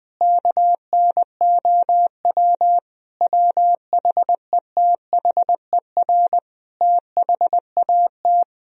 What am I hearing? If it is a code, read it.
Morse code, heard as a keyed tone, about 21 wpm, 700 Hz. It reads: KNOW WHETHER THAT
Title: Sentences from Top 1000 Words - Review Part 3 - 20wpm